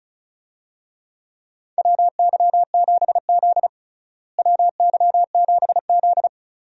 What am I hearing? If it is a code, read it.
Morse code, heard as a keyed tone, about 36 wpm, 700 Hz. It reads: WY7Z WY7Z
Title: USA Call Signs - 4 Characters - Repeated 2x - 35wpm